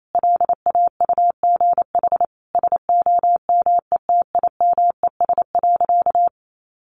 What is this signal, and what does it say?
Morse code, heard as a keyed tone, about 28 wpm, 700 Hz. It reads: LAUGH SOMETIMES.
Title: Sentences from Top 600 Words 28wpm